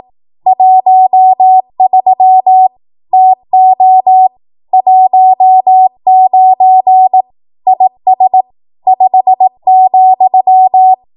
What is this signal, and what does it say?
Morse code, heard as a keyed tone, about 18 wpm, 750 Hz. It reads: 13 TO 19 IS 5,
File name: Morse / 18WPM